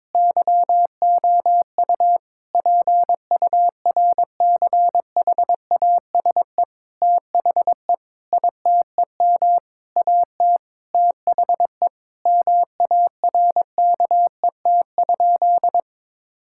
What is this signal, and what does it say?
Morse code, heard as a keyed tone, about 22 wpm, 700 Hz. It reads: YOU PURCHASE THE ITEM AT THE MARKET?
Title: Sentences from Top 1000 Words - Review Part 6 - 22wpm